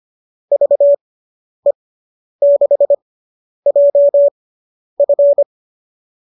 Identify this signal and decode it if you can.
Morse code, heard as a keyed tone, about 25 wpm, spaced as for 10 wpm, 590 Hz. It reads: VE6JF